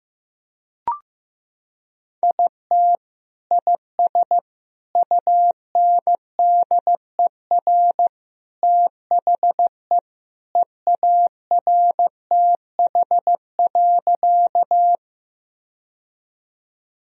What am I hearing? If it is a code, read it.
Morse code, heard as a keyed tone, about 15 wpm, 700 Hz. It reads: IT IS UNDER THE EARTH.
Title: Sentences from Top 1000 Words - Review Part 1 - 15wpm